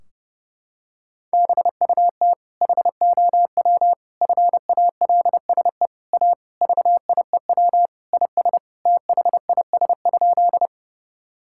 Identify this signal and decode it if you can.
Morse code, heard as a keyed tone, about 30 wpm, 700 Hz. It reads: BUT HOW FALSE A VIEW IS THIS?
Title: Sentences from Top 1600 Words 30wpm